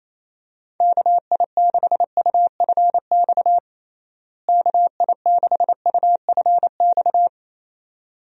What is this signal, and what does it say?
Morse code, heard as a keyed tone, about 28 wpm, 700 Hz. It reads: KI6UFX KI6UFX
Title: USA Call Signs - 6 Characters - Repeated 2x - 28wpm